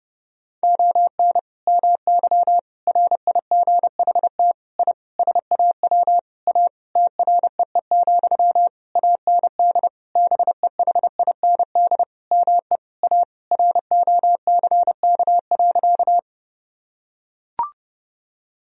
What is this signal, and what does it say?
Morse code, heard as a keyed tone, about 30 wpm, 700 Hz. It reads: ON MY RIGHT I SAW A TREE, AND BEHIND ME A ROCK.
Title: Sentences from Top 1000 Words - Review Part 2 - 30wpm